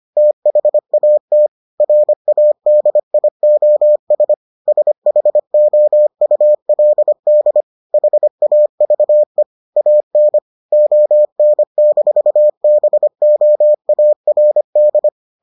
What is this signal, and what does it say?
Morse code, heard as a keyed tone, about 25 wpm, 600 Hz. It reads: THAT RADIOS SHOULD HAVE AN ON-BOARD